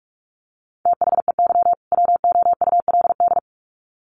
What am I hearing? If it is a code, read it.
Morse code, heard as a keyed tone, about 44 wpm, 700 Hz. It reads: THEY WOULD
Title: Sets of 2 Words 45wpm